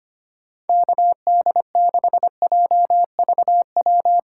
Morse code, heard as a keyed tone, about 25 wpm, 700 Hz. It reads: KD6JVW
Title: USA Call Signs - 4x Character Spacing - 25wpm